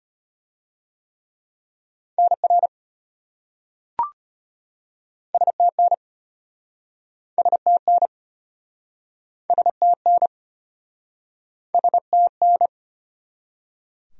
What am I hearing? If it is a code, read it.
Morse code, heard as a keyed tone, about 34 wpm, 700 Hz. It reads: NR STN STN STN STN